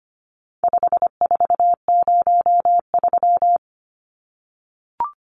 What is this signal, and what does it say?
Morse code, heard as a keyed tone, about 25 wpm, 700 Hz. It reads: 5403
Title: Numbers - 4-Digit 25wpm